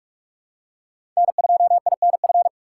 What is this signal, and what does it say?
Morse code, heard as a keyed tone, about 46 wpm, 700 Hz. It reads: N2INF
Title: USA Call Signs - 3x Character Spacing - 45wpm